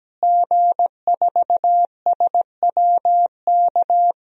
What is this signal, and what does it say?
Morse code, heard as a keyed tone, about 17 wpm, 700 Hz. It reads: G4SWK
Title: DXCC-100 Call Signs - 5 Characters - Repeated 3x - 17wpm